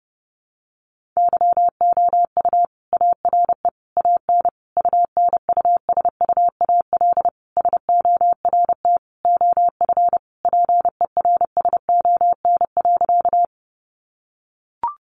Morse code, heard as a keyed tone, about 30 wpm, 700 Hz. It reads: YOU ARE AN UNUSUAL SORT OF PERSON.
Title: Sentences from Top 1400 Words 30wpm